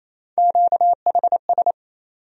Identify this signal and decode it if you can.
Morse code, heard as a keyed tone, about 28 wpm, 700 Hz. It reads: QHS
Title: USA Call Sign Postfixes - 3 Letters 28wpm